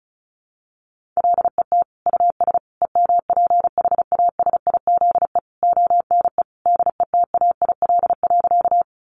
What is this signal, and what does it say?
Morse code, heard as a keyed tone, about 34 wpm, 700 Hz. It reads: LET US EMPHASIZE ONE DETAIL.
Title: Sentences from Top 1500 Words 35wpm